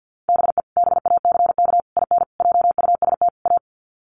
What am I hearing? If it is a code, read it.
Morse code, heard as a keyed tone, about 50 wpm, 700 Hz. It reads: BE BACK IN JUST A
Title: Sets of 5 Words Encore 50wpm